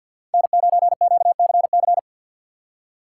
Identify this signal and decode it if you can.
Morse code, heard as a keyed tone, about 64 wpm, 690 Hz. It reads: N9QCC